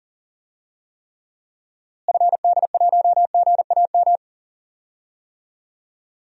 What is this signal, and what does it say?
Morse code, heard as a keyed tone, about 40 wpm, 700 Hz. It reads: FD1GAM
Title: DXCC-100 Call Signs - 6 Characters - Repeated 2x - 40wpm